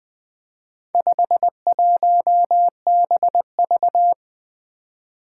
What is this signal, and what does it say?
Morse code, heard as a keyed tone, about 20 wpm, 700 Hz. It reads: 51BV
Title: Four Challenging Letter-Number Combinations 20wpm